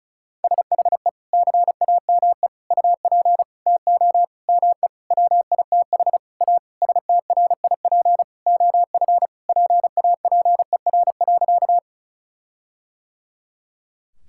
Morse code, heard as a keyed tone, about 36 wpm, 700 Hz. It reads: SHE CAME UP TO ME WITH A STRIP OF PAPER.
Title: Sentences from Top 1600 Words 35wpm